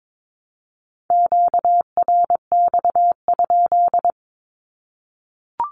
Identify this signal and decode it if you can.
Morse code, heard as a keyed tone, about 22 wpm, 700 Hz. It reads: QRX?
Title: Q-Codes 22wpm